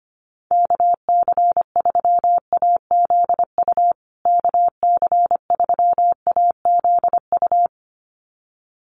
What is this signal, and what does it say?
Morse code, heard as a keyed tone, about 25 wpm, 700 Hz. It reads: KC3AZU KC3AZU
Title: USA Call Signs Repeated 2x 25wpm